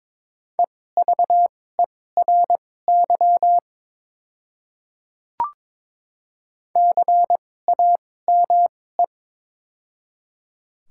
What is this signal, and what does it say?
Morse code, heard as a keyed tone, about 22 wpm, spaced as for 16 wpm, 690 Hz. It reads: EVERY CAME
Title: 2x Character Spacing - Top 200 Words - 22wpm